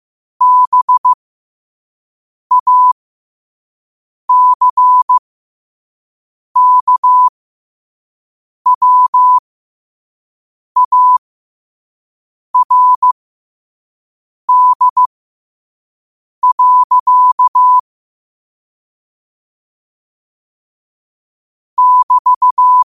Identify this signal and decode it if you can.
Morse code, heard as a keyed tone, about 15 wpm, spaced as for 5 wpm, 1000 Hz. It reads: BACKWARD. <BT>